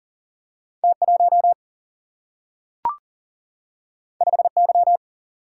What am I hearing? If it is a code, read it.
Morse code, heard as a keyed tone, about 38 wpm, 700 Hz. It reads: T1 5Y
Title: Single-Letter Single-Number 40wpm